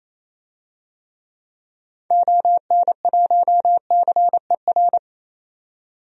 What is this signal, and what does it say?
Morse code, heard as a keyed tone, about 28 wpm, 700 Hz. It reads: ON1CER
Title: DXCC-100 Call Signs - 6 Characters - 28wpm